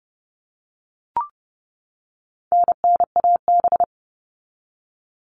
Morse code, heard as a keyed tone, about 30 wpm, 700 Hz. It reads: NNAB
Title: All Cut Numbers - 4-Digit 30wpm